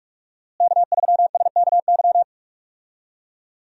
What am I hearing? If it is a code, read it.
Morse code, heard as a keyed tone, about 42 wpm, 700 Hz. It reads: K3SKY
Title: USA Call Signs - 5 Characters - Repeated 2x - 45wpm